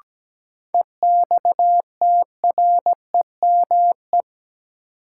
Morse code, heard as a keyed tone, about 17 wpm, 700 Hz. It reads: EXTREME
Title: Sentences from Top 1300 Words - New Word Drill Alt - 17wpm